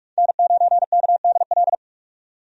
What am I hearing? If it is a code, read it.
Morse code, heard as a keyed tone, about 46 wpm, 700 Hz. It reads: N9KDL
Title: USA Call Signs - 5 Characters - 45wpm